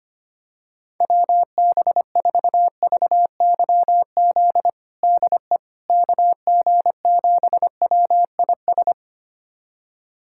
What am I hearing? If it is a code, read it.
Morse code, heard as a keyed tone, about 25 wpm, 700 Hz. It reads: WB4VYZ DE KG7WIS
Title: USA Callsign DE Callsign 25wpm